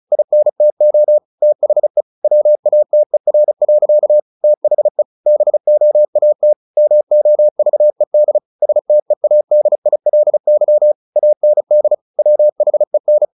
Morse code, heard as a keyed tone, about 34 wpm, 600 Hz. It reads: INTO THE WATER. THE BOAT MOVED STEADILY AND WHEN
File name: the old man and the sea / the old man and the sea 1 35wpm